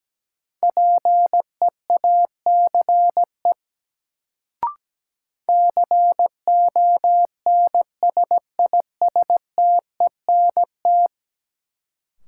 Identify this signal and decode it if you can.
Morse code, heard as a keyed tone, about 17 wpm, 700 Hz. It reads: PEACE CONSISTENT
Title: Sentences from Top 1500 Words - New Word Drill - 17wpm